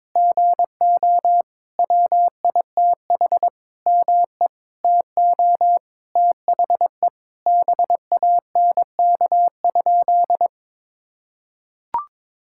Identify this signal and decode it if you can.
Morse code, heard as a keyed tone, about 22 wpm, 700 Hz. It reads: GO WITH ME TO THE BANK?
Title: Sentences from Top 1000 Words - Review Part 6 - 22wpm